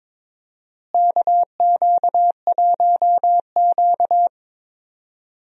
Morse code, heard as a keyed tone, about 22 wpm, 700 Hz. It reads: KQ1Q
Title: USA Call Signs - 4 Characters - Repeated 3x - 22wpm